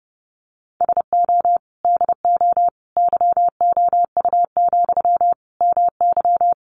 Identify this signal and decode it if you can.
Morse code, heard as a keyed tone, about 28 wpm, 700 Hz. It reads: SO DO YOU, MY
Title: Sentences from Top 400 Words 30wpm